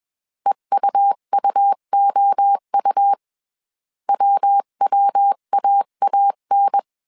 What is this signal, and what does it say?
Morse code, heard as a keyed tone, about 21 wpm, 800 Hz. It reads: EUUOU WWAAN
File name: 20wpm